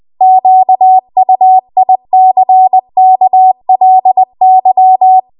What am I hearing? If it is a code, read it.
Morse code, heard as a keyed tone, about 20 wpm, 750 Hz. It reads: QUICKLY